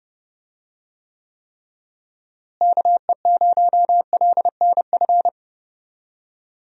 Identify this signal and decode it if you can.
Morse code, heard as a keyed tone, about 30 wpm, 700 Hz. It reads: KE0LNF
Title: USA Call Signs - 6 Characters - Repeated 3x - 30wpm